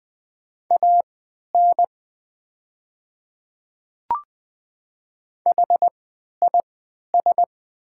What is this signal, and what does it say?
Morse code, heard as a keyed tone, about 20 wpm, spaced as for 11 wpm, 700 Hz. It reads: AN HIS
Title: 3x Character Spacing - Top 200 Words - 20wpm